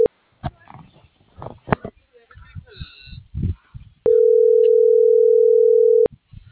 Ringback tone (North America), 440 + 480 Hz.